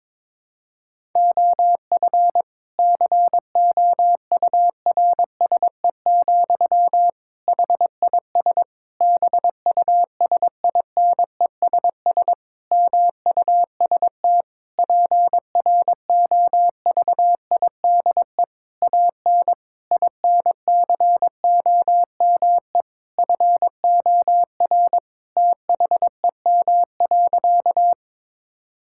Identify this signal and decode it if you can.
Morse code, heard as a keyed tone, about 22 wpm, 700 Hz. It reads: OF COURSE, HIS BUSINESS MUST PROVIDE AN INCOME FOR THEM.
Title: Sentences from Top 1000 Words - Review Part 5 - 22wpm